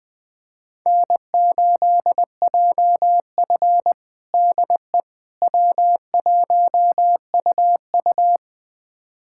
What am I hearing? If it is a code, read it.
Morse code, heard as a keyed tone, about 20 wpm, 700 Hz. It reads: N8JF DE W1UU
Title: USA Callsign DE Callsign - 4 Characters - 20wpm